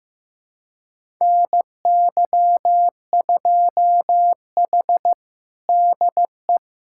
Morse code, heard as a keyed tone, about 16 wpm, 700 Hz. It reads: NY2H DE
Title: USA Callsign DE Callsign - 4 Characters - 15wpm